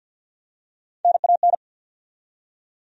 Morse code, heard as a keyed tone, about 48 wpm, 700 Hz. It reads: NAN
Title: USA Common Names - Max 4 Letters - 50wpm